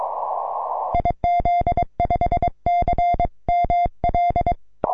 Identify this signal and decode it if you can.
Morse code, heard as a keyed tone, about 22 wpm, 2100 Hz. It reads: IZ5CML
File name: simple-file-list / 3B9FR 6M CW